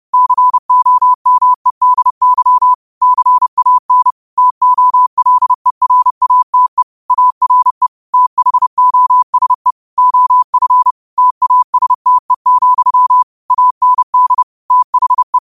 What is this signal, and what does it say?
Morse code, heard as a keyed tone, about 30 wpm, 1000 Hz. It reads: COMEDY CAN TOLERATE ARE THOSE OF TASTE, AND THE
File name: qotdfiles / 30wpm